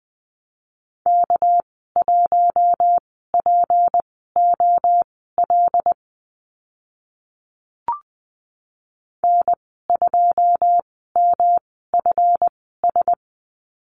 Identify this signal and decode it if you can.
Morse code, heard as a keyed tone, about 20 wpm, spaced as for 15 wpm, 700 Hz. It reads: K1POL N2MFS
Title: 231 Lesson 37g - Letter x - Call Signs 20wpm